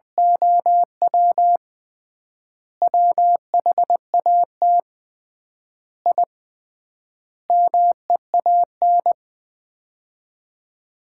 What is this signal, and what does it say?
Morse code, heard as a keyed tone, about 20 wpm, 700 Hz. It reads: OW WHAT I MEAN
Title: 3x Word-Spacing - Sets of 5 Words - 20wpm